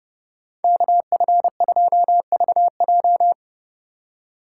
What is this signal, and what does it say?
Morse code, heard as a keyed tone, about 29 wpm, 700 Hz. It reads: KF2VJ